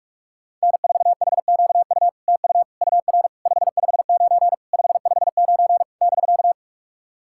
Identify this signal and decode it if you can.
Morse code, heard as a keyed tone, about 44 wpm, 700 Hz. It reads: N4HQU TU UR 559 559 <BK>
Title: SOTA QSO Practice 45wpm